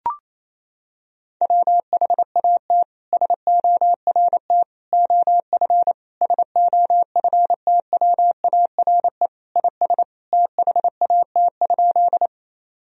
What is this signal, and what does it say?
Morse code, heard as a keyed tone, about 28 wpm, 700 Hz. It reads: WHAT SORT OF SOFTWARE IS THAT?
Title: Sentences from Top 1000 Words - Review Part 3 - 28wpm